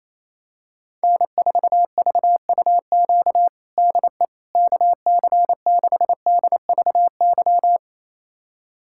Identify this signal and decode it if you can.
Morse code, heard as a keyed tone, about 28 wpm, 700 Hz. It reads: N4VUQ DE KC6DVY
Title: USA Callsign DE Callsign 28wpm